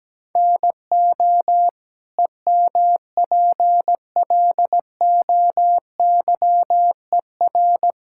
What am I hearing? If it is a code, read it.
Morse code, heard as a keyed tone, about 17 wpm, 700 Hz. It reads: NO EMPLOYER
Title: Sentences from Top 1600 Words 17wpm